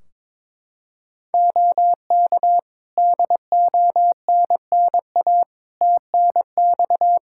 Morse code, heard as a keyed tone, about 22 wpm, 700 Hz. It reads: OK DONNA TNX
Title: Practice QSO - Exchange 4 - Ending - 22wpm